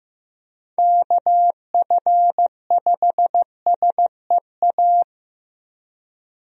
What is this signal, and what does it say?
Morse code, heard as a keyed tone, about 16 wpm, 700 Hz. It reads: KF5SEA
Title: USA Call Signs - 6 Characters - Repeated 3x - 15wpm